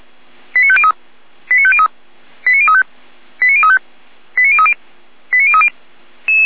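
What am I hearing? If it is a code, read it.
Five-tone selective call (ZVEI), heard as quick runs of five tones, tones of about 70 ms. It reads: 89503 89503 89035 89035 89030 89030
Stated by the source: ZVEI five-tone call received on SDR.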